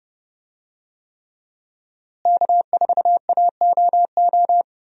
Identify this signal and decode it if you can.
Morse code, heard as a keyed tone, about 30 wpm, 700 Hz. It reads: K4AOO
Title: USA Call Signs - 5 Characters - Repeated 3x - 30wpm